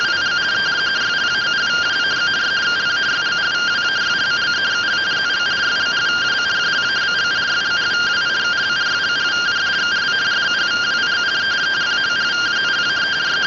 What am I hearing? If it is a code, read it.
Two-tone teleprinter signal, about 170 Hz shift.
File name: findings / CT 205 ax mips, iDose 3 0005.gen-29 rtty rx